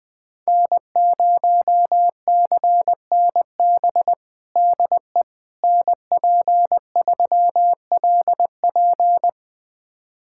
Morse code, heard as a keyed tone, about 20 wpm, 700 Hz. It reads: N0CNB DE NP3LP
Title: USA Callsign DE Callsign - 5 Characters - 20wpm